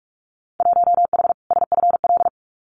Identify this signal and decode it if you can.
Morse code, heard as a keyed tone, about 44 wpm, 700 Hz. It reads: 1H SFL
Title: Winter Field Day Exchanges 45wpm